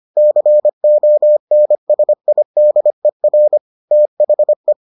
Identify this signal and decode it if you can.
Morse code, heard as a keyed tone, about 25 wpm, 600 Hz. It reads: CONSIDER THE